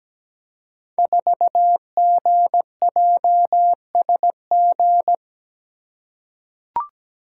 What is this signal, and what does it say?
Morse code, heard as a keyed tone, about 17 wpm, 700 Hz. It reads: 4GJSG
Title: Code Groups - 5 Letter:Numbers - 17wpm